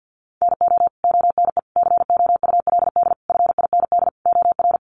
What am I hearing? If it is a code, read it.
Morse code, heard as a keyed tone, about 50 wpm, 700 Hz. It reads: NO ONE COULD FIND OR
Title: Sentences from Top 1100 Words 50wpm